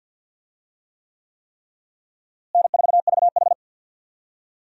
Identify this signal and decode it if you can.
Morse code, heard as a keyed tone, about 51 wpm, 700 Hz. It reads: N4VH
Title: USA Call Signs - 4 Characters - Repeated 3x - 50wpm